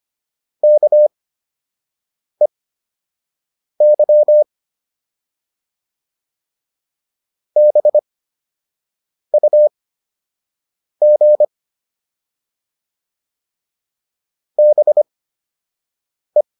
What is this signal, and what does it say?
Morse code, heard as a keyed tone, about 25 wpm, spaced as for 6 wpm, 600 Hz. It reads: KEY BUG BE